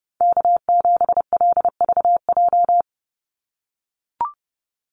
Morse code, heard as a keyed tone, about 30 wpm, 700 Hz. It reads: K7LVJ
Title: Call Signs - 30wpm